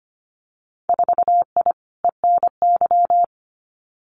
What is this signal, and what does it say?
Morse code, heard as a keyed tone, about 25 wpm, 700 Hz. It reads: 4I ENY
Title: Winter Field Day Exchanges Encore 25wpm